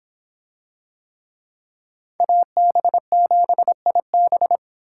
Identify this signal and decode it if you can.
Morse code, heard as a keyed tone, about 26 wpm, 700 Hz. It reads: AB7IB